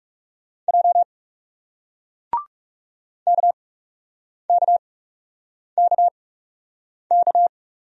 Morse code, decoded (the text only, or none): J KKKK